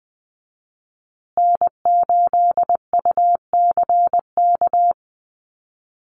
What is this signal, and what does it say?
Morse code, heard as a keyed tone, about 20 wpm, 700 Hz. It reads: N8UCK
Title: USA Call Signs Repeated 2x 20wpm